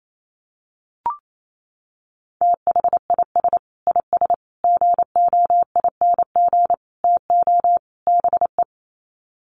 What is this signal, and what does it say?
Morse code, heard as a keyed tone, about 28 wpm, 700 Hz. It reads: THIS IS GOING TO BE
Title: Sets of 5 Words 28wpm